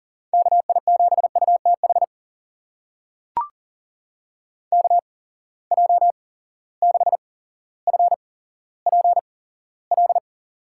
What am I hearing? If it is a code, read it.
Morse code, heard as a keyed tone, about 38 wpm, spaced as for 17 wpm, 700 Hz. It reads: KI7UTH KJ6FPL
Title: USA Call Signs - 6 Characters - 8x Character Spacing - 40wpm